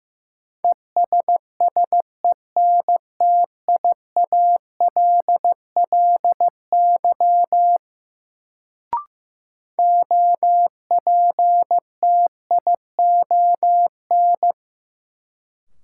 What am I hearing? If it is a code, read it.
Morse code, heard as a keyed tone, about 15 wpm, 700 Hz. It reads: ESSENTIALLY OPTION
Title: Sentences from Top 1300 Words - New Word Drill Alt - 15wpm